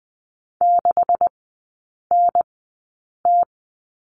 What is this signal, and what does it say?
Morse code, heard as a keyed tone, about 19 wpm, spaced as for 8 wpm, 700 Hz. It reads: 6NT
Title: 127 Lesson 21e - Number 6 - Three Characters 20wpm